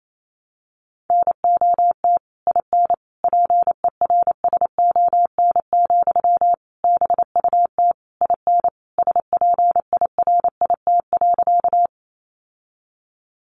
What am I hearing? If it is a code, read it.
Morse code, heard as a keyed tone, about 28 wpm, 700 Hz. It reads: NOT IN PERSON, BUT IN SPIRIT.
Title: Sentences from Top 1400 Words 28wpm